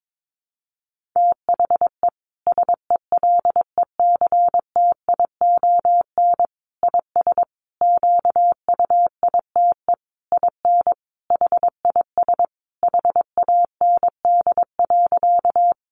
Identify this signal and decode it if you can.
Morse code, heard as a keyed tone, about 22 wpm, 700 Hz. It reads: THE SELECTION IS QUITE IN HIS HAND.
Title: Sentences from Top 1400 Words 22wpm